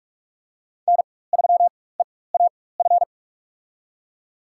Morse code, heard as a keyed tone, about 43 wpm, spaced as for 21 wpm, 700 Hz. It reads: N3EAF
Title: USA Call Signs - 5 Characters - 4x Character Spacing - 45wpm